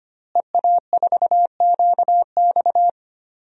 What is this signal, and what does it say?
Morse code, heard as a keyed tone, about 25 wpm, 700 Hz. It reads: EA4QX
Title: DXCC-100 Call Signs - 5 Characters - Repeated 3x - 25wpm